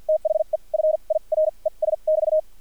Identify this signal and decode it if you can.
Morse code, heard as a keyed tone, about 50 wpm, spaced as for 38 wpm, 620 Hz. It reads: THE FIRES<BT>
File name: archived / CRSnet